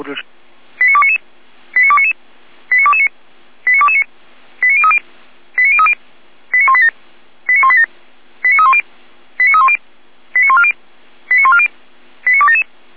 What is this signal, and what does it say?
Five-tone selective call (ZVEI), heard as quick runs of five tones, tones of about 80 ms, with speech. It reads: 89299 89299 89229 89229 89030 89030 89187 89187 89310 89310 89140 89140 89277